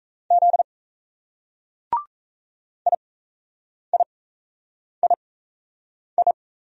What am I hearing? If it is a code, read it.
Morse code, heard as a keyed tone, about 38 wpm, spaced as for 8 wpm, 700 Hz. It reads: Z IIII